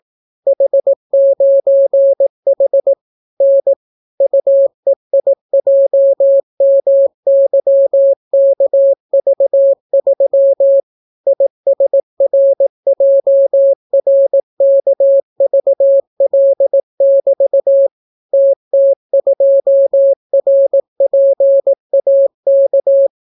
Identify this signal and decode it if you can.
Morse code, heard as a keyed tone, about 18 wpm, 550 Hz. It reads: H9H N UEIJMYKV3 ISRJRKVL<BT> TT2RPAK